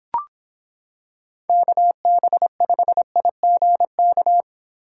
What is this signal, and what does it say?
Morse code, heard as a keyed tone, about 26 wpm, 700 Hz. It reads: KB5IGK